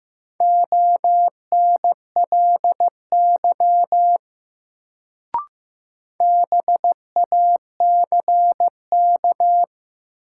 Morse code, heard as a keyed tone, about 15 wpm, 700 Hz. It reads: ONLY BACK
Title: Sentences from Top 200 Words - New Word Drill - 15wpm